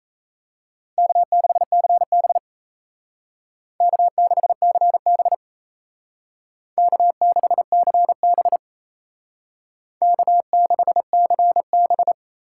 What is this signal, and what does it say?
Morse code, heard as a keyed tone, about 36 wpm, 700 Hz. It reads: K6CB K6CB K6CB K6CB